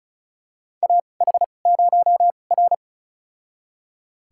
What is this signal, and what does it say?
Morse code, heard as a keyed tone, about 36 wpm, spaced as for 25 wpm, 700 Hz. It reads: AH0R
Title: USA Call Signs - 4 Characters - 2x Character Spacing - 35wpm